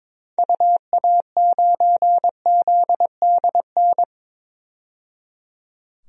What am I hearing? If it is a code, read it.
Morse code, heard as a keyed tone, about 22 wpm, 700 Hz. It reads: UA9ZDN
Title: DXCC-100 Call Signs - 6 Characters - 22wpm